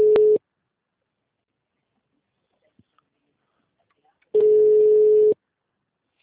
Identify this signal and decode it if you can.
Ringback tone (Europe, 425 Hz).
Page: Rio 3 - Ura